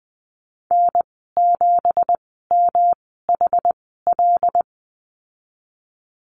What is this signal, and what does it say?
Morse code, heard as a keyed tone, about 20 wpm, spaced as for 15 wpm, 700 Hz. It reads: N7MHL
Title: 200 Lesson 32g - Letter k - Call Signs 20wpm